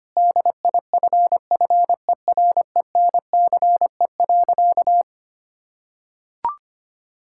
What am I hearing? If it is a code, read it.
Morse code, heard as a keyed tone, about 25 wpm, 700 Hz. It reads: DIFFERENCE.
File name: practice / 25wpm / Sentences with 3-Word Phrases - 25wpm